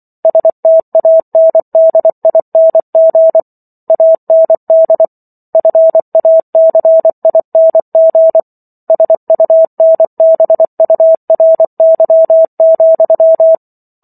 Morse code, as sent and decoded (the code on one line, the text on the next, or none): ... - .- -. -.. .. -. --. / .- -. -.. / ..-. .- -.-. .. -. --. / ... ..- -. -... ..- .-. -.-- --..--
STANDING AND FACING SUNBURY,